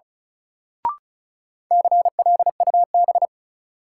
Morse code, heard as a keyed tone, about 36 wpm, 700 Hz. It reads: CLUB
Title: Sentences from Top 1200 Words - New Word Drill Alt - 35wpm